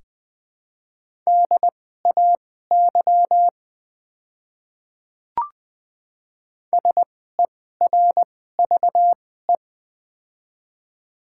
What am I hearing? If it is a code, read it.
Morse code, heard as a keyed tone, about 21 wpm, spaced as for 15 wpm, 690 Hz. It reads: DAY SERVE
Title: 2x Character Spacing - Top 500 Words - 20wpm